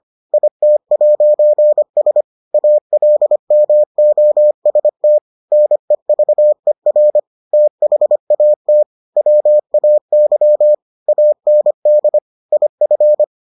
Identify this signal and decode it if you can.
Morse code, heard as a keyed tone, about 25 wpm, 600 Hz. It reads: IT'S ALMOST NEVER THAT WAY AND IF